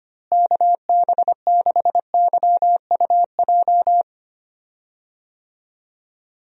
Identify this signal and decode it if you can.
Morse code, heard as a keyed tone, about 25 wpm, 700 Hz. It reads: KB6YUJ
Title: USA Call Signs - 6 Characters - Repeated 2x - 25wpm